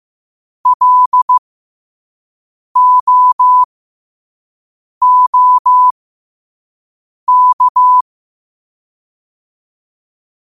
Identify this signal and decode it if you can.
Morse code, heard as a keyed tone, about 15 wpm, spaced as for 5 wpm, 1000 Hz. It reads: LOOK
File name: 5wpm